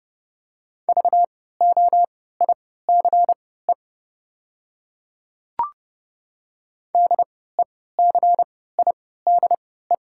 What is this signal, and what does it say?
Morse code, heard as a keyed tone, about 30 wpm, spaced as for 17 wpm, 700 Hz. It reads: VOICE DECIDE
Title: 3x Character Spacing - Top 500 Words - 30wpm